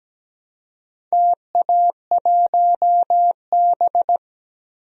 Morse code, heard as a keyed tone, about 17 wpm, 700 Hz. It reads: TA1B
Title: DXCC-100 Call Signs - 4 Characters - Repeated 3x - 17wpm